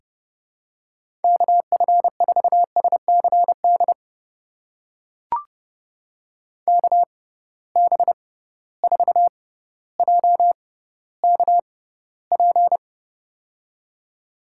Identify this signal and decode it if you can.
Morse code, heard as a keyed tone, about 32 wpm, spaced as for 16 wpm, 700 Hz. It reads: KF4SCD KB4JKP
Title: USA Call Signs - 6 Characters - 6x Character Spacing - 30wpm